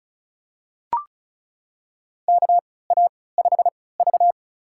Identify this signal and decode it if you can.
Morse code, heard as a keyed tone, about 36 wpm, spaced as for 20 wpm, 700 Hz. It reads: KA5V
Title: USA Call Signs - 3x Character Spacing - 35wpm